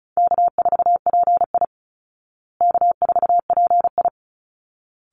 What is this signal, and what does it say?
Morse code, heard as a keyed tone, about 34 wpm, 700 Hz. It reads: K4PI K4PI
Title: Call Signs Repeated 3x - 35wpm